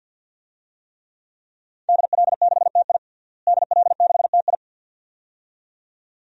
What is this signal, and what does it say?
Morse code, heard as a keyed tone, about 52 wpm, 700 Hz. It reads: DL6TI DL6TI
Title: DXCC-100 Call Signs - 5 Characters - Repeated 2x - 50wpm